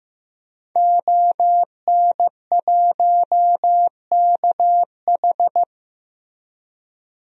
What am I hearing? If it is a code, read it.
Morse code, heard as a keyed tone, about 16 wpm, 700 Hz. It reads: ON1KH
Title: DXCC-100 Call Signs - 5 Characters - Repeated 3x - 15wpm